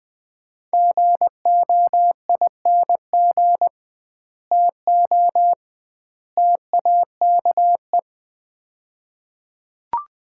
Morse code, heard as a keyed tone, about 20 wpm, 700 Hz. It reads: GOING TO TAKE
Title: 2x Word-Spacing - Sets of 3 Words - 20wpm